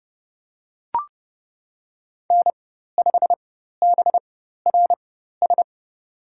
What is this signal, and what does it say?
Morse code, heard as a keyed tone, about 29 wpm, spaced as for 14 wpm, 700 Hz. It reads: N5BRS
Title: USA Call Signs - 4x Character Spacing - 30wpm